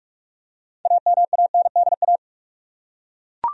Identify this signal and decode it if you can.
Morse code, heard as a keyed tone, about 41 wpm, 700 Hz. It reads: AMANDA
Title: USA Common Names - 45wpm